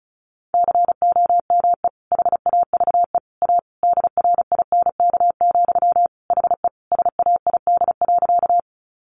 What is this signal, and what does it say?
Morse code, heard as a keyed tone, about 34 wpm, 700 Hz. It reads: COME HAVE A DRINK, HE SAID.
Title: Sentences from Top 1500 Words 35wpm